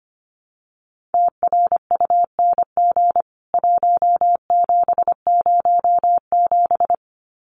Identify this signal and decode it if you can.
Morse code, heard as a keyed tone, about 25 wpm, 700 Hz. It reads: TRUNG 1707
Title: CWT Exchange 25wpm